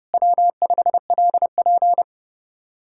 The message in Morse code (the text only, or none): W5LP